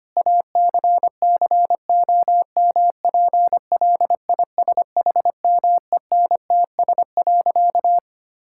Morse code, heard as a keyed tone, about 25 wpm, 700 Hz. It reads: ACCOMPLISHMENTS.
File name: practice / 25wpm / Difficult Sentences with Hints - 25wpm